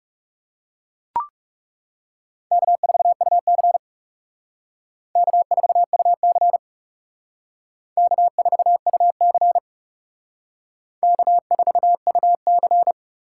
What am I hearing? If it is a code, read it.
Morse code, heard as a keyed tone, about 36 wpm, 700 Hz. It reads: K4UC K4UC K4UC K4UC
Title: Speed Racing - USA 1x2 Call Signs 30wpm